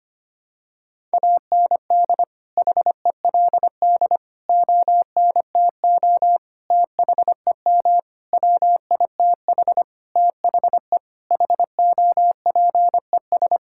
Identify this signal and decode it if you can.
Morse code, heard as a keyed tone, about 25 wpm, 700 Hz. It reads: AND HELD ONTO THEM WITH THE HOPES
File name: Sets of 9 Words - 25wpm